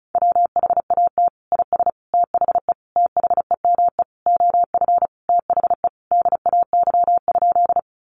Morse code, heard as a keyed tone, about 34 wpm, 700 Hz. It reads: WHAT IS THE THEME OF THE DAY?
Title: Sentences from Top 1100 Words 35wpm